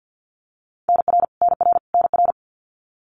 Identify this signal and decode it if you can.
Morse code, heard as a keyed tone, about 50 wpm, 700 Hz. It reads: NR NR NR
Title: Abbreviations Repeated 3x - 50wpm